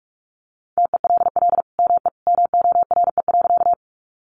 Morse code, heard as a keyed tone, about 44 wpm, 700 Hz. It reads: TELL ME MORE.
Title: Sentences from Top 300 Words 45wpm